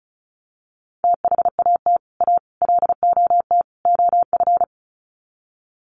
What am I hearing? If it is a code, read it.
Morse code, heard as a keyed tone, about 34 wpm, 700 Hz. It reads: THAT A LOT OF
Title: Sets of 4 Words 35wpm